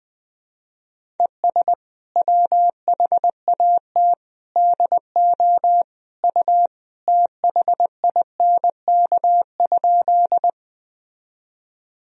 Morse code, heard as a keyed tone, about 21 wpm, 700 Hz. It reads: ES WHAT DO U THINK?
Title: Sentences from Top 200 Words and Abbreviations 20wpm